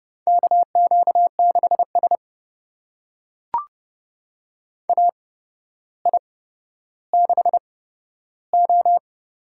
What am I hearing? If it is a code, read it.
Morse code, heard as a keyed tone, about 29 wpm, spaced as for 13 wpm, 700 Hz. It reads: KQ6S AI6O
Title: USA Call Signs - 4 Characters - 8x Character Spacing - 30wpm